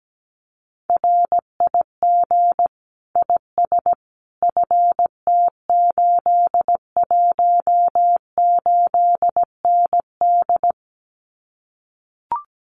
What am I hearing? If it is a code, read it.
Morse code, heard as a keyed tone, about 17 wpm, 700 Hz. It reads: RIG IS FT818ND
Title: Common QSO Phrases Encore 17wpm